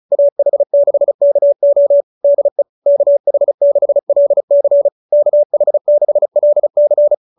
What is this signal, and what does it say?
Morse code, heard as a keyed tone, about 34 wpm, 570 Hz. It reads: AH6KO DE KH6LC KH6LC